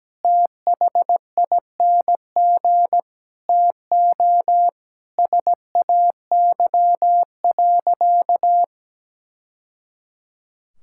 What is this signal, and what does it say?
Morse code, heard as a keyed tone, about 17 wpm, 700 Hz. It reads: THING TO SAY.
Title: Sentences from Top 1000 Words - Review Part 5 - 17wpm